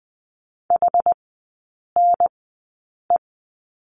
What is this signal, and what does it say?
Morse code, heard as a keyed tone, about 20 wpm, spaced as for 8 wpm, 700 Hz. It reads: HNE
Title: 060 Lesson 11e - Letter h - Three Characters 20wpm